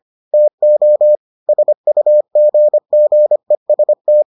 Morse code, heard as a keyed tone, about 25 wpm, 600 Hz. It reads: TO SUGGEST